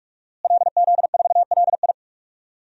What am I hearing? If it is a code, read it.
Morse code, heard as a keyed tone, about 41 wpm, 700 Hz. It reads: LZ4LI
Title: DXCC-100 Call Signs - 5 Characters - Repeated 3x - 45wpm